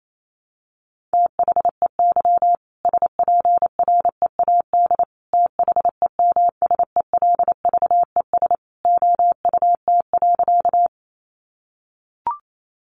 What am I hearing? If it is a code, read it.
Morse code, heard as a keyed tone, about 28 wpm, 700 Hz. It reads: THEY SPREAD THEMSELVES OUT.
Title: Sentences from Top 1200 Words 28wpm